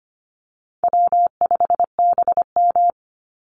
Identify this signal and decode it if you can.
Morse code, heard as a keyed tone, about 25 wpm, 700 Hz. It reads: W5BM
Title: Easy Call Signs - 25wpm